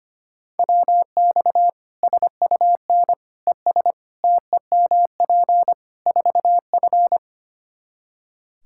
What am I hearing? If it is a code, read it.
Morse code, heard as a keyed tone, about 25 wpm, 700 Hz. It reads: WX SUN ES TEMP 4F
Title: Practice QSO - Exchange 2 - Info - 25wpm